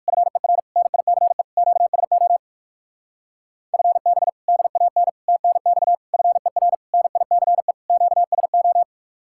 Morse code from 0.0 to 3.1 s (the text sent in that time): FER NICE QSO